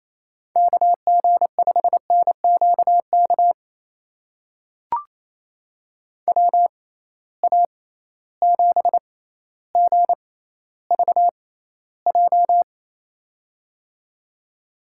Morse code, heard as a keyed tone, about 28 wpm, spaced as for 14 wpm, 700 Hz. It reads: KG5NQK WA7GVJ
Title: USA Call Signs - 6 Characters - 6x Character Spacing - 28wpm